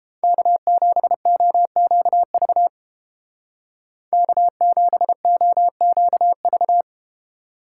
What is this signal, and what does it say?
Morse code, heard as a keyed tone, about 31 wpm, 700 Hz. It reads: K7OQV K7OQV